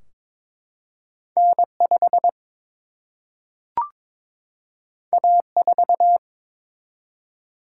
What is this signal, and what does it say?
Morse code, heard as a keyed tone, about 23 wpm, 700 Hz. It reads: N5 A4
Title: Single-Letter Single-Number 22wpm